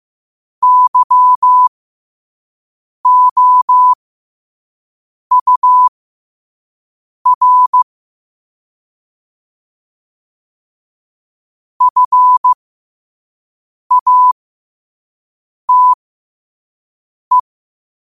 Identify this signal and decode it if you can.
Morse code, heard as a keyed tone, about 15 wpm, spaced as for 5 wpm, 1000 Hz. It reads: YOUR FATE
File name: qotdfiles / 5wpm